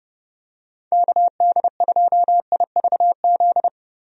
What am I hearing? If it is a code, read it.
Morse code, heard as a keyed tone, about 30 wpm, 700 Hz. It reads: KD2IVZ
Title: USA Call Signs - 6 Characters - Repeated 2x - 30wpm